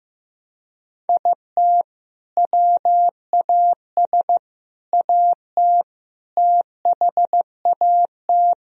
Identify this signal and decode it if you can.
Morse code, heard as a keyed tone, about 16 wpm, 700 Hz. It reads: IT WAS AT THAT
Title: Sentences from Top 1000 Words - Review Part 3 - 15wpm